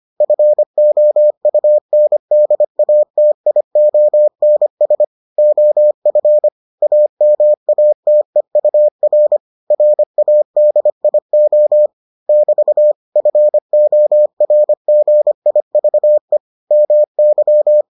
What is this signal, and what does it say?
Morse code, heard as a keyed tone, about 25 wpm, 600 Hz. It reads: FOUNDATIONS OF AMATEUR RADIO <BT> FORGIVE MY